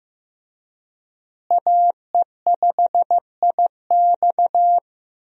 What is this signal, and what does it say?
Morse code, heard as a keyed tone, about 15 wpm, 700 Hz. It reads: AE5IX
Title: USA Call Signs - 5 Characters - Repeated 3x - 15wpm